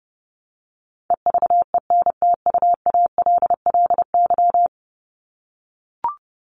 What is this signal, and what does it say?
Morse code, heard as a keyed tone, about 30 wpm, 700 Hz. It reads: EVENTUALLY
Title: LY Suffix - Top 100 Words with Base Words 30wpm